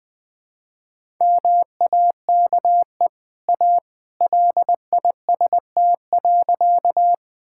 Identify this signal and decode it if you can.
Morse code, heard as a keyed tone, about 21 wpm, 700 Hz. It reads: MAKE A LIST.
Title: Easy Sentences with Hints - 20wpm